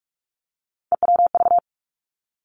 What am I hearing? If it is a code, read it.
Morse code, heard as a keyed tone, about 46 wpm, 700 Hz. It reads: EWV
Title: Three Challenging Letter-Number Combinations 45wpm